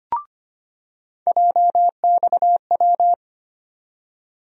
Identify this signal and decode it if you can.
Morse code, heard as a keyed tone, about 25 wpm, 700 Hz. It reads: JXW
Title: USA Call Sign Postfixes - 3 Letters 25wpm